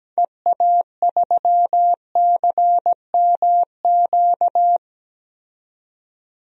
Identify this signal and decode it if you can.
Morse code, heard as a keyed tone, about 17 wpm, 700 Hz. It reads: EA3CMQ
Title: DXCC-100 Call Signs - 6 Characters - 17wpm